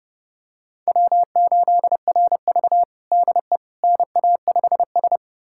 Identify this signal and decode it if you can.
Morse code, heard as a keyed tone, about 30 wpm, 700 Hz. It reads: W8RV DE NA5S
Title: USA Callsign DE Callsign - 4 Characters - 30wpm